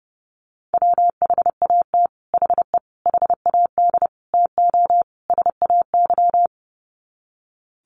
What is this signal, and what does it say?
Morse code, heard as a keyed tone, about 30 wpm, 700 Hz. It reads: WHAT HE HAD TO SAY
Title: Sets of 5 Words Encore2 30wpm